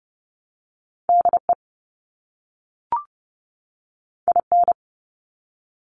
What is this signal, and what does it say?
Morse code, heard as a keyed tone, about 30 wpm, 700 Hz. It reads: DE IN
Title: ARRL Sections 30wpm